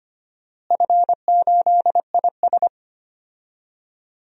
Morse code, heard as a keyed tone, about 25 wpm, 700 Hz. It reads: F8IS
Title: DXCC-100 Call Signs - 4 Characters - Repeated 3x - 25wpm